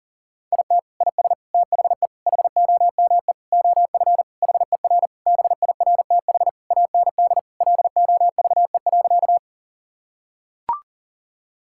Morse code, heard as a keyed tone, about 37 wpm, 700 Hz. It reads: IT IS THE HOME OF HER BIRTH AND LOVE.
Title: Sentences from Top 1000 Words - Review Part 5 - 40wpm